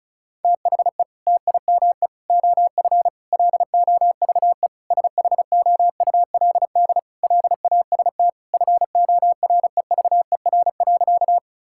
Morse code, heard as a keyed tone, about 34 wpm, 700 Hz. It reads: THE TIME OF LOVE SHOULD LAST FOREVER.
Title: Sentences from Top 1600 Words 35wpm